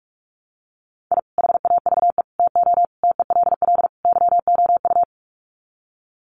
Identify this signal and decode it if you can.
Morse code, heard as a keyed tone, about 44 wpm, 700 Hz. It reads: I HAVE TO TELL YOU
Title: Sets of 5 Words 45wpm